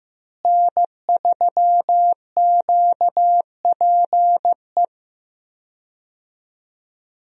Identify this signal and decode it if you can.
Morse code, heard as a keyed tone, about 15 wpm, 700 Hz. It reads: N3QPE
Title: USA Call Signs - 5 Characters - Repeated 2x - 15wpm